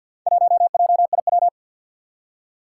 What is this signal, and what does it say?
Morse code, heard as a keyed tone, about 46 wpm, 700 Hz. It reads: 1JIW
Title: Four Challenging Letter-Number Combinations 50wpm